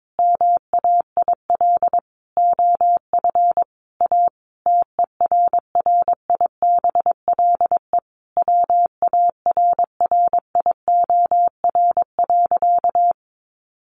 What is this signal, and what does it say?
Morse code, heard as a keyed tone, about 22 wpm, 700 Hz. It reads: MAIL OF A TERRIBLE WARRIOR.
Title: A Sound Of Thunder 22wpm